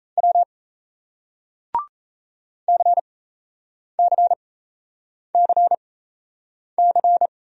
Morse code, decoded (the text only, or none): W CCCC